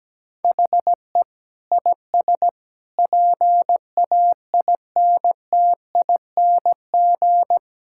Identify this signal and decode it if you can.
Morse code, heard as a keyed tone, about 17 wpm, 700 Hz. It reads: HE IS PAINTING
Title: Sentences from Top 1000 Words - Review Part 6 - 17wpm